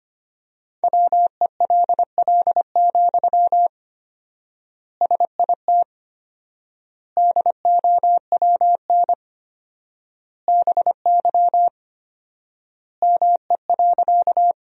Morse code, heard as a keyed tone, about 25 wpm, 700 Hz. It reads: WELL, SIT DOWN BY ME.
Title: 4x Word-Spacing - Sentences from Top 500 Words - 25wpm